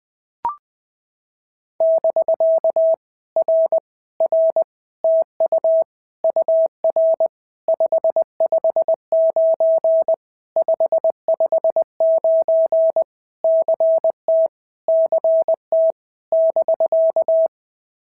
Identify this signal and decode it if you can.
Morse code, heard as a keyed tone, about 20 wpm, 650 Hz. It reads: <BK> R R TU UR 559 559 CT CT <BK>
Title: POTA QSO Practice - 20wpm